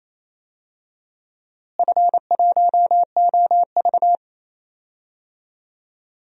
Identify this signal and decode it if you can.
Morse code, heard as a keyed tone, about 28 wpm, 700 Hz. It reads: F1OV
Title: DXCC-100 Call Signs - 4 Characters - Repeated 3x - 28wpm